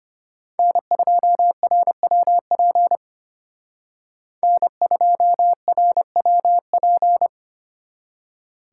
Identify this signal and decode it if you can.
Morse code, heard as a keyed tone, about 30 wpm, 700 Hz. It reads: N2RWP N2RWP
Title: Speed Racing - USA 1x3 Call Signs 25wpm